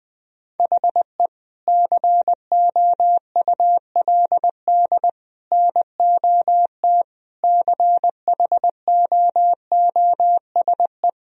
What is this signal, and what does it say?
Morse code, heard as a keyed tone, about 21 wpm, 700 Hz. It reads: HE COULD NOT CHOOSE
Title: Sentences from Top 1000 Words - Review Part 4 - 20wpm